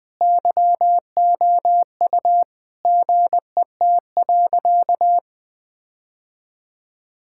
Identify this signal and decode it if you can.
Morse code, heard as a keyed tone, about 20 wpm, 700 Hz. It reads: YOU GET.
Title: Sentences with 7-Word Phrases - 20wpm